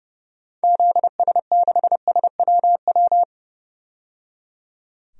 Morse code, heard as a keyed tone, about 30 wpm, 700 Hz. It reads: ZS6SWW
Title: DXCC-100 Call Signs - 6 Characters - 30wpm